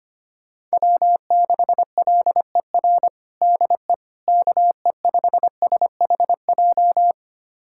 Morse code, heard as a keyed tone, about 25 wpm, 700 Hz. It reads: W6LER DE KE5SHJ
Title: USA Callsign DE Callsign 25wpm